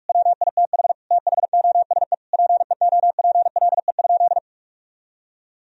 Morse code, heard as a keyed tone, about 46 wpm, 700 Hz. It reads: WITH THOSE PEOPLE?
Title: Sentences from Top 1000 Words - Review Part 1 - 45wpm